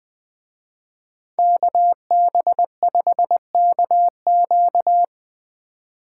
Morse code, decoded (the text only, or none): KB5KQ